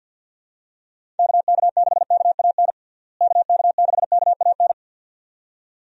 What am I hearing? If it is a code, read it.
Morse code, heard as a keyed tone, about 52 wpm, 700 Hz. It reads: KK6KAN KK6KAN
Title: USA Call Signs - 6 Characters - Repeated 2x - 50wpm